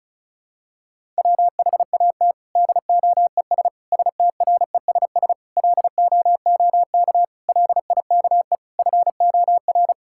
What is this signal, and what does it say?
Morse code, heard as a keyed tone, about 36 wpm, 700 Hz. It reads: WHAT DOES STRESS LOOK LIKE FOR
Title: Sentences from Top 1000 Words - Review Part 6 - 35wpm